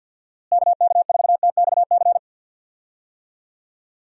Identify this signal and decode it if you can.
Morse code, heard as a keyed tone, about 46 wpm, 700 Hz. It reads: KK4TXC
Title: USA Call Signs - 6 Characters - Repeated 3x - 50wpm